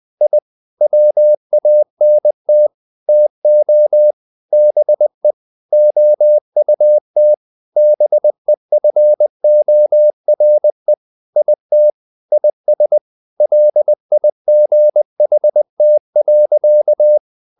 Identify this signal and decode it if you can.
Morse code, heard as a keyed tone, about 20 wpm, 600 Hz. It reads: I WANT TO BE OUT BEFORE IT IS LIGHT.
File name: the old man and the sea / the old man and the sea part 1 20wpm